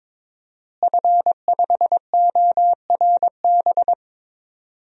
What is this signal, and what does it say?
Morse code, heard as a keyed tone, about 22 wpm, 700 Hz. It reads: F5ORB
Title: DXCC-100 Call Signs - 5 Characters - Repeated 2x - 22wpm